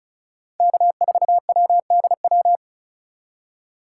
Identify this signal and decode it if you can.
Morse code, heard as a keyed tone, about 36 wpm, 700 Hz. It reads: K4WDW
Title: USA Call Signs - 5 Characters - 2x Character Spacing - 35wpm